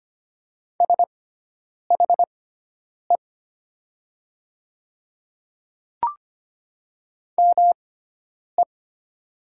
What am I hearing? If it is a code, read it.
Morse code, heard as a keyed tone, about 25 wpm, spaced as for 9 wpm, 700 Hz. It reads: SHE ME
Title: 6x Character Spacing - Top 200 Words - 25wpm